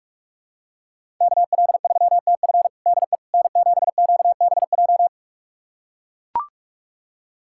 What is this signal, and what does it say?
Morse code, heard as a keyed tone, about 42 wpm, 700 Hz. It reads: KL3TF DE N7QBJ
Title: USA Callsign DE Callsign - 5 Characters - 45wpm